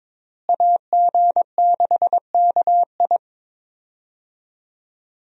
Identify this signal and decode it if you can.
Morse code, heard as a keyed tone, about 22 wpm, 700 Hz. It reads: AG6KI
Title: USA Call Signs - 5 Characters - Repeated 3x - 22wpm